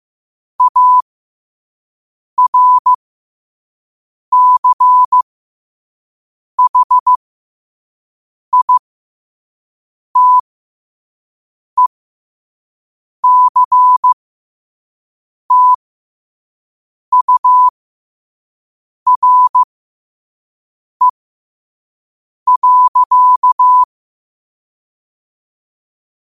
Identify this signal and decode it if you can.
Morse code, heard as a keyed tone, about 15 wpm, spaced as for 5 wpm, 1000 Hz. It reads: ARCHITECTURE.
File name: qotdfiles / 5wpm